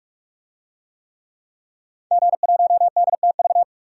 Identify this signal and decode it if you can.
Morse code, heard as a keyed tone, about 47 wpm, 700 Hz. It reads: G1DTV
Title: DXCC-100 Call Signs - 5 Characters - 45wpm